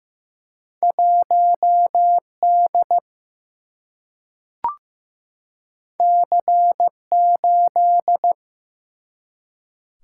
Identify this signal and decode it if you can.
Morse code, heard as a keyed tone, about 15 wpm, 700 Hz. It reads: 1D C8
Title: Single-Letter Single-Number 15wpm